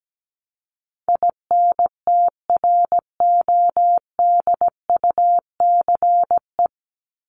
Morse code, heard as a keyed tone, about 17 wpm, 700 Hz. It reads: INTRODUCE
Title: Top 1501-1750 Words 17wpm